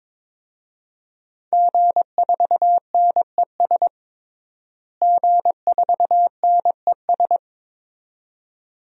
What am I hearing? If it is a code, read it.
Morse code, heard as a keyed tone, about 22 wpm, 700 Hz. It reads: G4NES G4NES
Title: DXCC-100 Call Signs - 5 Characters - Repeated 2x - 22wpm